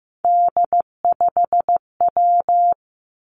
Morse code, heard as a keyed tone, about 15 wpm, 700 Hz. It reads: D5W
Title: Three Challenging Letter-Number Combinations 15wpm